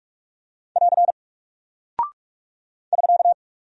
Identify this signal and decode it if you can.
Morse code, heard as a keyed tone, about 44 wpm, 700 Hz. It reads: <AR> <SK>
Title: Single Character (Letter:Number:Punctuation:Prosign) Alt 45wpm